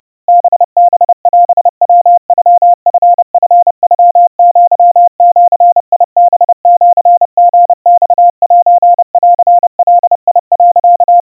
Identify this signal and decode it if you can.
Morse code (Japanese Wabun code), heard as a keyed tone, about 30 wpm, 700 Hz. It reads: ハハオヤノチチノアジハシリマセンガ、
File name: kawa / Morse sample / kamyu 110 cpm 700 Hz Crispy